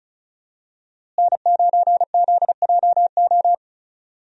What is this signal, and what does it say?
Morse code, heard as a keyed tone, about 35 wpm, 700 Hz. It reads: N9ZJO
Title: USA Call Signs - 5 Characters - Repeated 3x - 35wpm